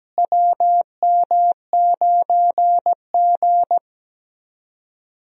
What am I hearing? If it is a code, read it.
Morse code, heard as a keyed tone, about 17 wpm, 700 Hz. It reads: WM9G
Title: USA Call Signs - 4 Characters - Repeated 3x - 17wpm